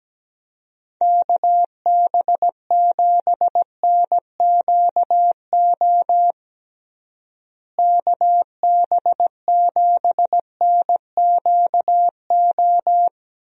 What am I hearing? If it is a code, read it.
Morse code, heard as a keyed tone, about 17 wpm, 700 Hz. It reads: KB7NQO KB7NQO
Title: USA Call Signs - 6 Characters - Repeated 3x - 17wpm